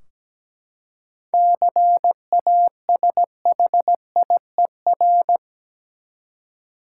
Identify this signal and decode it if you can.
Morse code, heard as a keyed tone, about 17 wpm, 700 Hz. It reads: CASHIER
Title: Practice QSO - Exchange 3 - More Info - 17wpm